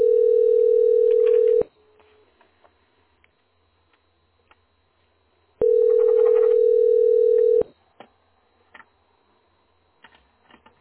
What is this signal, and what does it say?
Ringback tone (North America), 440 + 480 Hz.